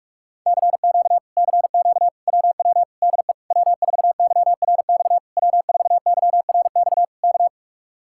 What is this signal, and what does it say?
Morse code, heard as a keyed tone, about 49 wpm, 700 Hz. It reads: CQ CQ WW DE W4YRX W4YRX K